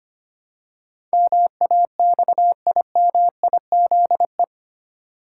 Morse code, heard as a keyed tone, about 25 wpm, 700 Hz. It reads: MAXIMIZE
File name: Eight Letter Words - 25wpm